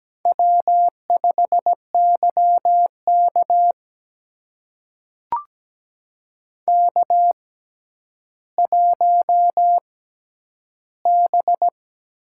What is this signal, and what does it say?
Morse code, heard as a keyed tone, about 17 wpm, spaced as for 10 wpm, 700 Hz. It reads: W5YK K1B
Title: USA Call Signs - 4 Characters - 6x Character Spacing - 17wpm